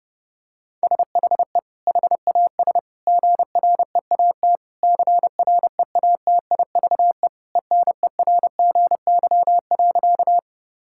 Morse code, heard as a keyed tone, about 32 wpm, 700 Hz. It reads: SHE HAS GREAT CREATIVE ENERGY.
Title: Sentences from Top 1000 Words - Review Part 6 - 30wpm